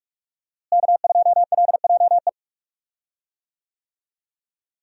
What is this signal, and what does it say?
Morse code, heard as a keyed tone, about 41 wpm, 700 Hz. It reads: K2LJE
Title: USA Call Signs - 5 Characters - Repeated 3x - 45wpm